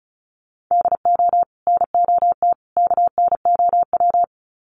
Morse code, heard as a keyed tone, about 34 wpm, 700 Hz. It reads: DO NOT KNOW
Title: Sets of 3 Words 35wpm